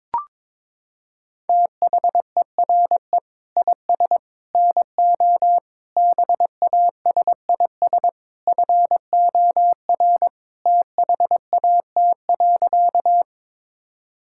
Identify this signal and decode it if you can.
Morse code, heard as a keyed tone, about 22 wpm, 700 Hz. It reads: THERE IS NO BASIS FOR THAT.
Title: Sentences from Top 1000 Words - Review Part 5 - 22wpm